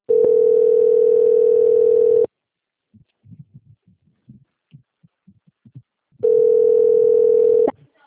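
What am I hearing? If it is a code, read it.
Ringback tone (North America), 440 + 480 Hz.
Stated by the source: Outgoing telephone call recorded on the Atibaia - SP 1 (CP Cartao) system.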